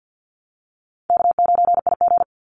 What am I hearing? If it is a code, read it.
Morse code, heard as a keyed tone, about 50 wpm, 700 Hz. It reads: K9IG
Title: Call Signs - 50wpm